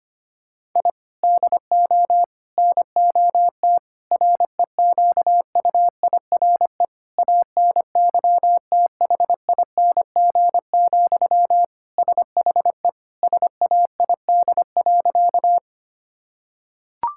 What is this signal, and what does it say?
Morse code, heard as a keyed tone, about 25 wpm, 700 Hz. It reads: I DO NOT REQUIRE ANYTHING, SHE SAID.
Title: Sentences from Top 1000 Words - Review Part 6 - 25wpm